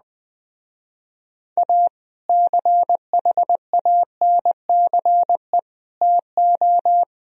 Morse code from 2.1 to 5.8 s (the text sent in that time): CHANCE